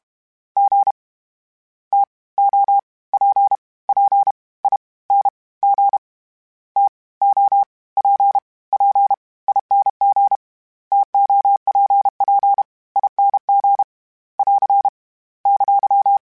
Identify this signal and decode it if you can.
Morse code, heard as a keyed tone, about 32 wpm, 800 Hz. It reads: G T O P P I N G T O P P ING TOPP ING <AR> !
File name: Misc / Misc-mp3s / TO @30